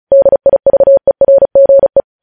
Morse code, heard as a keyed tone, about 36 wpm, 550 Hz. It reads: DIVERGE
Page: news headlines in morse code at 35 wpm